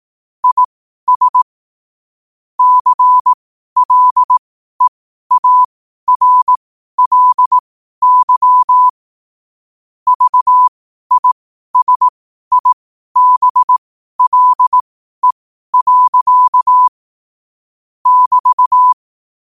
Morse code, heard as a keyed tone, about 18 wpm, spaced as for 12 wpm, 1000 Hz. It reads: IS CLEARLY VISIBLE. <BT>
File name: qotdfiles / 12wpm